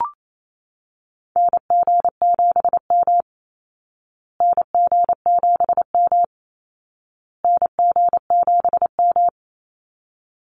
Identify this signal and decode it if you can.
Morse code, heard as a keyed tone, about 28 wpm, 700 Hz. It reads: NG7M NG7M NG7M
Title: Call Signs Repeated 2x - 28wpm